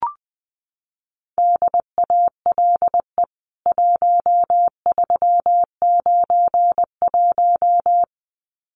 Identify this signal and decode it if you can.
Morse code, heard as a keyed tone, about 20 wpm, 700 Hz. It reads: DALE 1391
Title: CWT Exchange 20wpm